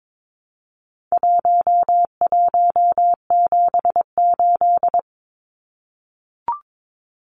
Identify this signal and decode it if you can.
Morse code, heard as a keyed tone, about 22 wpm, 700 Hz. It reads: 1178
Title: Numbers - 4-Digit 22wpm